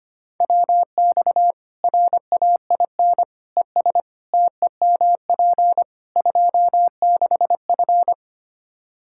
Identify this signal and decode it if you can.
Morse code, heard as a keyed tone, about 25 wpm, 700 Hz. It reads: WX RAIN ES TEMP 26F
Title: Practice QSO - Exchange 2 - Info - 25wpm